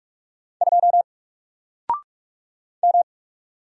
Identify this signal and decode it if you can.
Morse code, heard as a keyed tone, about 45 wpm, 700 Hz. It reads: 2 M
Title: Single Character (Letter:Number:Punctuation:Prosign) Alt 45wpm